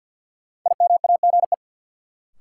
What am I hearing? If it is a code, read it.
Morse code, heard as a keyed tone, about 51 wpm, 700 Hz. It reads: IMAGE
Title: Sentences from Top 700 Words - New Word Drill - 50wpm